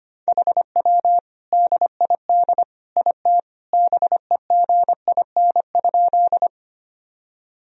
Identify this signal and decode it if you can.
Morse code, heard as a keyed tone, about 25 wpm, 700 Hz. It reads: HW DID IT BEGIN?
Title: Sentences from Top 300 Words and Abbreviations 25wpm